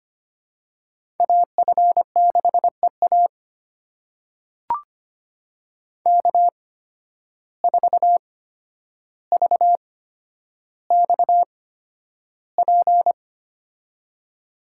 Morse code, heard as a keyed tone, about 25 wpm, spaced as for 11 wpm, 700 Hz. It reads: AF6EA K4VXP
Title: USA Call Signs - 5 Characters - 8x Character Spacing - 25wpm